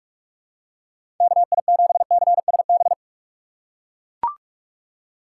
Morse code, heard as a keyed tone, about 41 wpm, 690 Hz. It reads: KI7CSB
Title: USA Call Signs - 6 Characters - 3x Character Spacing - 45wpm